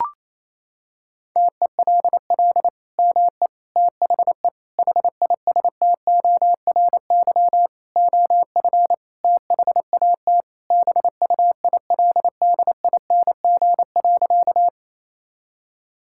Morse code, heard as a keyed tone, about 27 wpm, 700 Hz. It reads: TELL ME THE HISTORY OF THAT BUILDING.
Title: Sentences from Top 1000 Words - Review Part 5 - 28wpm